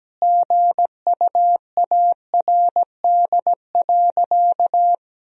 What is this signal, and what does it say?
Morse code, heard as a keyed tone, about 17 wpm, 700 Hz. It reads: GUARD.
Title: Sentences from Top 1000 Words - Review Part 4 - 17wpm